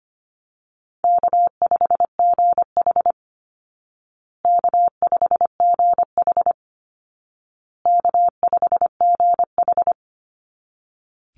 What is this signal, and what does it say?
Morse code, heard as a keyed tone, about 25 wpm, 700 Hz. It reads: K5GH K5GH K5GH
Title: Easy Call Signs Repeated 3x - 25wpm